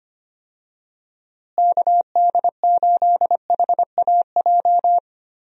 Morse code, heard as a keyed tone, about 25 wpm, 700 Hz. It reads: KD8HAJ
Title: USA Call Signs - 6 Characters - Repeated 3x - 25wpm